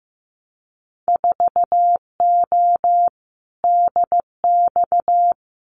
Morse code, heard as a keyed tone, about 15 wpm, 700 Hz. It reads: 4O DX
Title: Winter Field Day Exchanges Encore 15wpm